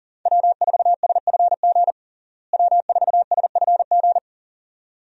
Morse code, heard as a keyed tone, about 40 wpm, 700 Hz. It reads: W4SFG W4SFG
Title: USA Call Signs - 5 Characters - Repeated 2x - 40wpm